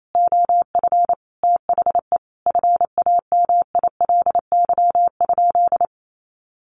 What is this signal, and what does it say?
Morse code, heard as a keyed tone, about 28 wpm, 700 Hz. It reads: OF THE FAMILY?
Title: Sentences from Top 1100 Words 28wpm